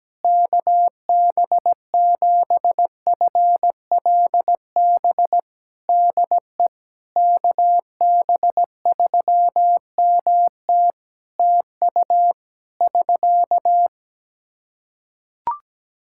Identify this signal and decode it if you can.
Morse code, heard as a keyed tone, about 17 wpm, 700 Hz. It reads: KB7FLB DE KB3MT TU <SK>
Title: Practice QSO - Exchange 4 - Ending - 17wpm